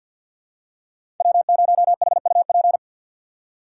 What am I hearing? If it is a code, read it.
Morse code, heard as a keyed tone, about 48 wpm, 700 Hz. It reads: W0HUP
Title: USA Call Signs - 5 Characters - 2x Character Spacing - 50wpm